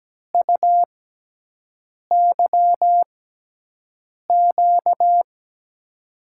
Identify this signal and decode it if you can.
Morse code, heard as a keyed tone, about 17 wpm, spaced as for 6 wpm, 700 Hz. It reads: UYQ
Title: USA Call Signs - 5 Characters - 6x Character Spacing - 17wpm